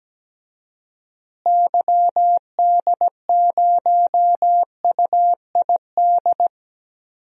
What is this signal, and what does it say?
Morse code, heard as a keyed tone, about 17 wpm, 700 Hz. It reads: YD0UID
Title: DXCC-100 Call Signs - 6 Characters - Repeated 3x - 17wpm